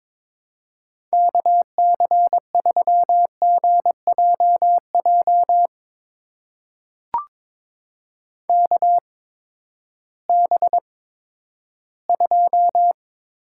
Morse code, heard as a keyed tone, about 22 wpm, spaced as for 12 wpm, 700 Hz. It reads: KC3GJJ KB2
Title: USA Call Signs - 6 Characters - 8x Character Spacing - 22wpm